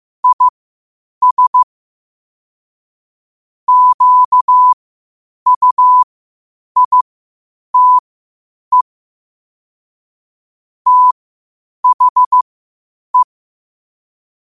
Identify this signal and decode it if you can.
Morse code, heard as a keyed tone, about 15 wpm, spaced as for 8 wpm, 1000 Hz. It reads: IS QUITE THE
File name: qotdfiles / 8wpm